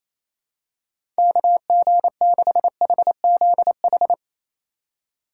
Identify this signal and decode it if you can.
Morse code, heard as a keyed tone, about 28 wpm, 700 Hz. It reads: KG6HZH
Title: USA Call Signs - 6 Characters - 2x Character Spacing - 28wpm